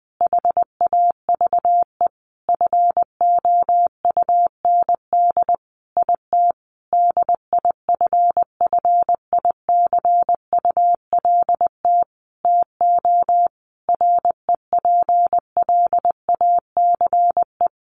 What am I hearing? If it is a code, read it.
Morse code, heard as a keyed tone, about 20 wpm, 700 Hz. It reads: HAVE FOUND IT DIFFICULT TO REPLACE
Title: Sentences from Top 1400 Words 20wpm